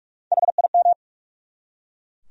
Morse code, heard as a keyed tone, about 43 wpm, 700 Hz. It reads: HIM
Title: Top 100 Words 45wpm